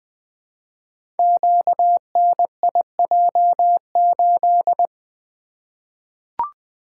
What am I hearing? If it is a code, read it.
Morse code, heard as a keyed tone, about 20 wpm, 700 Hz. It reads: QNIJ8
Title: Code Groups - 5 Letter:Numbers - 20wpm